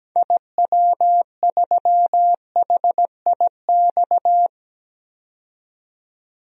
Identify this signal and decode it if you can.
Morse code, heard as a keyed tone, about 17 wpm, 700 Hz. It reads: IW3HIX
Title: DXCC-100 Call Signs - 6 Characters - 17wpm